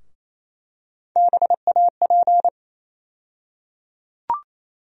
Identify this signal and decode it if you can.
Morse code, heard as a keyed tone, about 28 wpm, 700 Hz. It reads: BAP
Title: USA Call Sign Postfixes - 3 Letters 28wpm